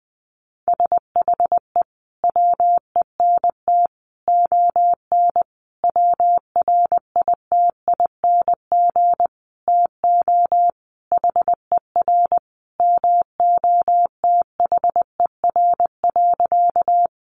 Morse code, heard as a keyed tone, about 20 wpm, 700 Hz. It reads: SHE WENT ON WRITING TO HER MOTHER.
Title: Sentences from Top 800 Words 20wpm